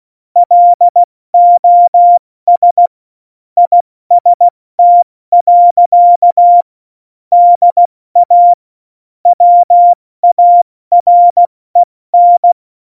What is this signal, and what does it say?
Morse code, heard as a keyed tone, about 16 wpm, 700 Hz. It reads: LOS IST. DA WAREN
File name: Morsekurs / Witze / Witz 24. Oktober 17, Tempo 70